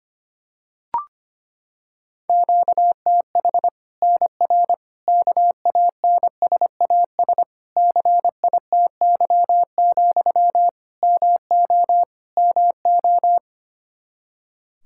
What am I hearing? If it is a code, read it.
Morse code, heard as a keyed tone, about 25 wpm, 700 Hz. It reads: QTH NR KANSAS CITY, MO MO
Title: Practice QSO - Exchange 1 - Beginning - 25wpm